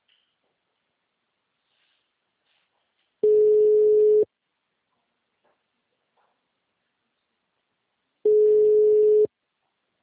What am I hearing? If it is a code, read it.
Ringback tone (Europe, 425 Hz).
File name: out- -2013